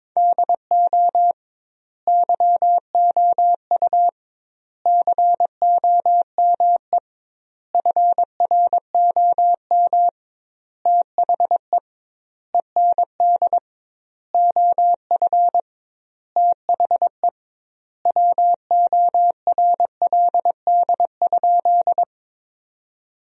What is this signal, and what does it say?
Morse code, heard as a keyed tone, about 22 wpm, 700 Hz. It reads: DO YOU COME FROM THE END OF THE WORLD?
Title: 2x Word-Spacing - Sentences from Top 200 Words - 22wpm